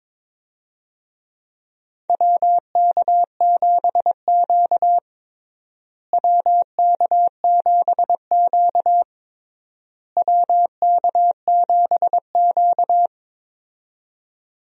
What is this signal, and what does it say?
Morse code, heard as a keyed tone, about 21 wpm, 700 Hz. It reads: WK7Q WK7Q WK7Q
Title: USA Call Signs - 4 Characters - Repeated 3x - 22wpm